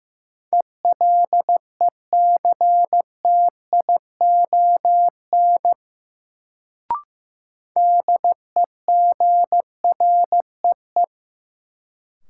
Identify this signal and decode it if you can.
Morse code, heard as a keyed tone, about 15 wpm, 700 Hz. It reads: ELECTION DEGREE
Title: Sentences from Top 1500 Words - New Word Drill Alt - 15wpm